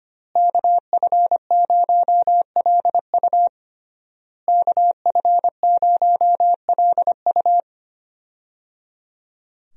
Morse code, heard as a keyed tone, about 25 wpm, 700 Hz. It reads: KF0LU KF0LU
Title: USA Call Signs - 5 Characters - Repeated 3x - 25wpm